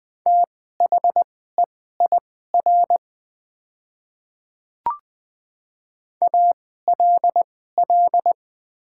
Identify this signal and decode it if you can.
Morse code, heard as a keyed tone, about 20 wpm, spaced as for 15 wpm, 700 Hz. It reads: THEIR ALL
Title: 2x Character Spacing - Top 100 Words - 20wpm